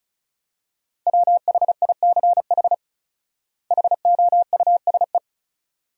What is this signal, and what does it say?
Morse code, heard as a keyed tone, about 36 wpm, 700 Hz. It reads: WHICH HOUSE
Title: 4x Word-Spacing - Sentences from Top 200 Words - 35wpm